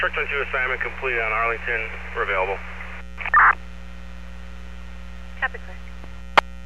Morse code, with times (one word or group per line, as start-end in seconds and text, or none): none